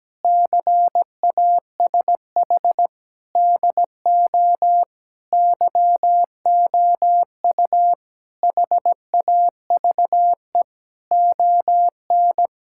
Morse code, heard as a keyed tone, about 17 wpm, 700 Hz. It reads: CASH DO YOU HAVE ON
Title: Sentences from Top 1000 Words - Review Part 5 - 17wpm